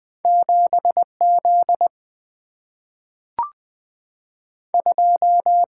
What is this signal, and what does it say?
Morse code, heard as a keyed tone, about 20 wpm, 700 Hz. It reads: 7 Z 2
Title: Single-Letter Single-Number 20wpm